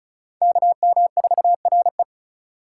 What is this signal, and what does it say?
Morse code, heard as a keyed tone, about 45 wpm, 700 Hz. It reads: KM4RE